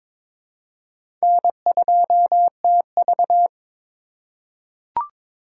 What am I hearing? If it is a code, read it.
Morse code, heard as a keyed tone, about 21 wpm, 700 Hz. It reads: N2TV
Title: USA Call Signs - 4 Characters - 2x Character Spacing - 22wpm